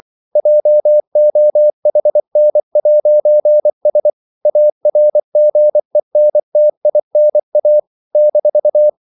Morse code, heard as a keyed tone, about 24 wpm, 600 Hz. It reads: JOHN'S ARGENTINA -
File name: CW / Practice / NationalCapitals24